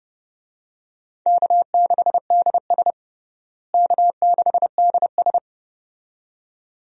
Morse code, heard as a keyed tone, about 30 wpm, 700 Hz. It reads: K6DS K6DS
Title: USA Call Signs - 4 Characters - Repeated 2x - 30wpm